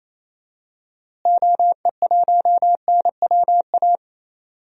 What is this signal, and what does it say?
Morse code, heard as a keyed tone, about 27 wpm, 700 Hz. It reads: OE1NWA
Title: DXCC-100 Call Signs - 6 Characters - Repeated 2x - 28wpm